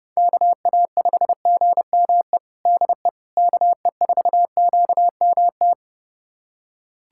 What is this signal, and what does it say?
Morse code, heard as a keyed tone, about 30 wpm, 700 Hz. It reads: KA5GME DE KE4QMT
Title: USA Callsign DE Callsign - 6 Characters - 30wpm